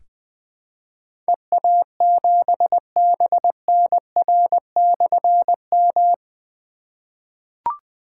Morse code, heard as a keyed tone, about 20 wpm, 700 Hz. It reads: EA7BNR/M
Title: Difficult Call Signs 20wpm